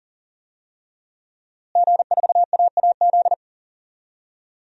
Morse code, heard as a keyed tone, about 40 wpm, 700 Hz. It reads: G4AAZ
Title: DXCC-100 Call Signs - 5 Characters - Repeated 2x - 40wpm